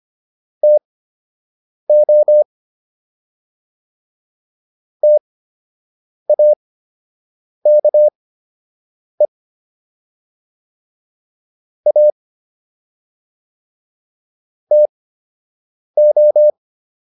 Morse code, heard as a keyed tone, about 25 wpm, spaced as for 7 wpm, 600 Hz. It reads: TO TAKE A TO